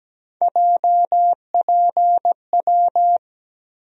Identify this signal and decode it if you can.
Morse code, heard as a keyed tone, about 17 wpm, 700 Hz. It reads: JPW
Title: USA Call Sign Postfixes - 3 Letters 17wpm